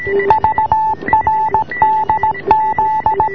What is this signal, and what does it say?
Morse code, heard as a keyed tone, about 17 wpm, 860 Hz, with noise. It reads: VRDZ